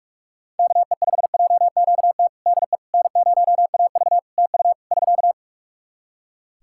Morse code, heard as a keyed tone, about 42 wpm, 700 Hz. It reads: KE5JQT DE N0AV TU <SK>
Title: Practice QSO - Exchange 4 - Ending - 45wpm